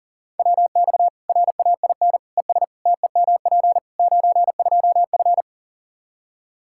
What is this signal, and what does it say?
Morse code, heard as a keyed tone, about 40 wpm, 700 Hz. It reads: WX RAIN ES TEMP 92F
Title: Practice QSO - Exchange 2 - Info - 40wpm